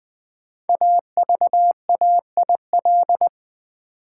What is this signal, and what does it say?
Morse code, heard as a keyed tone, about 20 wpm, 700 Hz. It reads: AVAIL
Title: Build A Word - 5 Letters At A Time - 12-Letter Words - 20wpm